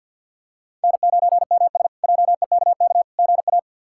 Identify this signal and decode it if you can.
Morse code, heard as a keyed tone, about 48 wpm, 700 Hz. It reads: N9MS JEKK GA
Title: K1USN SST QSO Practice - 50wpm